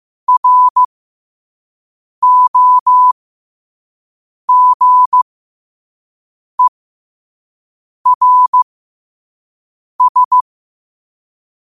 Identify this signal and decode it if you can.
Morse code, heard as a keyed tone, about 15 wpm, spaced as for 5 wpm, 1000 Hz. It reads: ROGERS
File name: qotdfiles / 5wpm